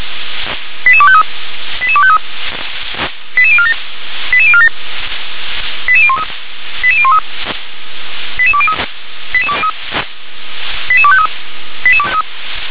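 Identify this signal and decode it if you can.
Five-tone selective call (ZVEI), heard as quick runs of five tones, tones of about 80 ms. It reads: 88253 88253 88047 88047 88013 88013 88203 88253
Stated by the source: ZVEI five-tone call 88203 received by SDR.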